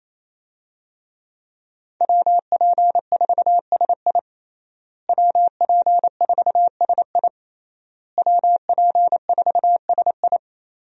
Morse code, heard as a keyed tone, about 27 wpm, 700 Hz. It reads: WP4SI WP4SI WP4SI
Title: USA Call Signs - 5 Characters - Repeated 3x - 28wpm